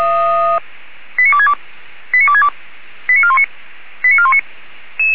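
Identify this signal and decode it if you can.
Five-tone selective call (ZVEI), heard as quick runs of five tones, tones of about 70 ms. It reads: 89262 89262 89419 89419
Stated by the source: ZVEI five-tone selective call received on SDR.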